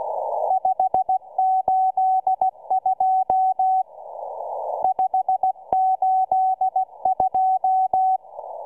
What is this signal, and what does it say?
Morse code, heard as a keyed tone, about 16 wpm, 760 Hz. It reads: 582 582